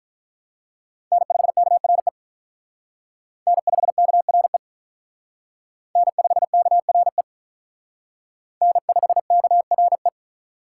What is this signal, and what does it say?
Morse code, heard as a keyed tone, about 44 wpm, 700 Hz. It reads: N5KRE N5KRE N5KRE N5KRE